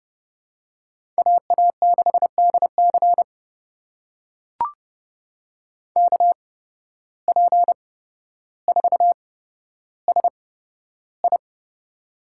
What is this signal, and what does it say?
Morse code, heard as a keyed tone, about 30 wpm, spaced as for 13 wpm, 700 Hz. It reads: AA6DC KP4SI
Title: USA Call Signs - 8x Character Spacing - 30wpm